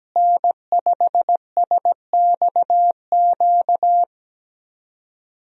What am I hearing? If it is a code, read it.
Morse code, heard as a keyed tone, about 17 wpm, 700 Hz. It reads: N5SXQ
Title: USA Call Signs - 5 Characters - Repeated 2x - 17wpm